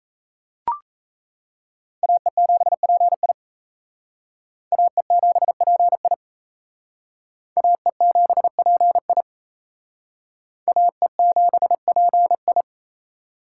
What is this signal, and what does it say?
Morse code, heard as a keyed tone, about 36 wpm, 700 Hz. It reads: AE7PI AE7PI AE7PI AE7PI